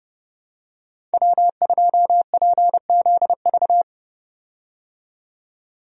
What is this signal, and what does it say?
Morse code, heard as a keyed tone, about 32 wpm, 700 Hz. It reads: W2PZV
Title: USA Call Signs - 5 Characters - 30wpm